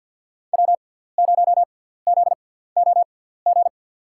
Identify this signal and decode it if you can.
Morse code, heard as a keyed tone, about 49 wpm, spaced as for 17 wpm, 700 Hz. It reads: W0ZOG
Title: USA Call Signs - 5 Characters - 6x Character Spacing - 50wpm